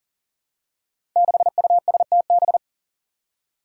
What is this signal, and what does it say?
Morse code, heard as a keyed tone, about 40 wpm, 700 Hz. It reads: 6USTB
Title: Five Challenging Letter-Number Combinations 40wpm